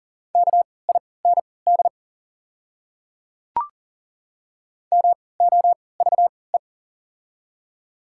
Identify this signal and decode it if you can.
Morse code, heard as a keyed tone, about 40 wpm, spaced as for 23 wpm, 700 Hz. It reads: KIND MOVE
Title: 3x Character Spacing - Top 200 Words - 40wpm